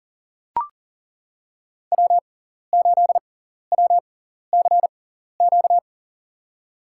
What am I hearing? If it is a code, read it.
Morse code, heard as a keyed tone, about 37 wpm, spaced as for 14 wpm, 700 Hz. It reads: W8WCQ
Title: USA Call Signs - 6x Character Spacing - 40wpm